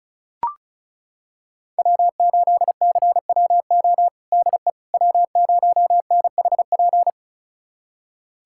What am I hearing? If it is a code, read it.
Morse code, heard as a keyed tone, about 36 wpm, 690 Hz. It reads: W8CWO DE W0NHP
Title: USA Callsign DE Callsign - 5 Characters - 35wpm